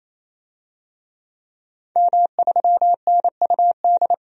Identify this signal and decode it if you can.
Morse code, heard as a keyed tone, about 28 wpm, 700 Hz. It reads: M3NUD
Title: DXCC-100 Call Signs - 5 Characters - 28wpm